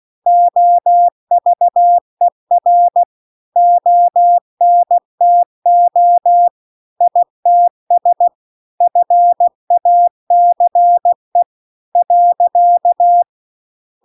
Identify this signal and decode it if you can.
Morse code, heard as a keyed tone, about 16 wpm, 700 Hz. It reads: OVER ONTO ITS FACE .
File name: CW-The-Big-Twoheaded-River-2-16WPM